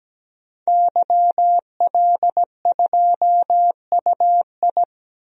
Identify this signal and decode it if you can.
Morse code, heard as a keyed tone, about 17 wpm, 700 Hz. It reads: YL2UI
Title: DXCC-100 Call Signs - 5 Characters - 17wpm